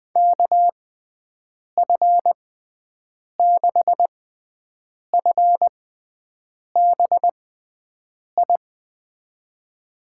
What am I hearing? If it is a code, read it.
Morse code, heard as a keyed tone, about 20 wpm, spaced as for 7 wpm, 700 Hz. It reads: KF6FBI
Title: USA Call Signs - 6 Characters - 6x Character Spacing - 20wpm